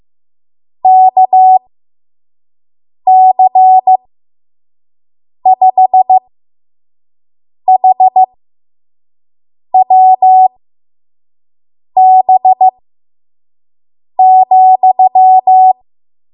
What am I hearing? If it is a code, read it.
Morse code, heard as a keyed tone, about 15 wpm, spaced as for 5 wpm, 750 Hz. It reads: KC5HWB,
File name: Morse / 05WPM